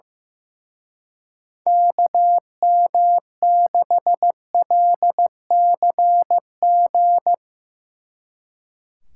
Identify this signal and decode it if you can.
Morse code, heard as a keyed tone, about 15 wpm, 700 Hz. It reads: KM6LCG
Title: Speed Racing - USA 2x3 Call Signs 15wpm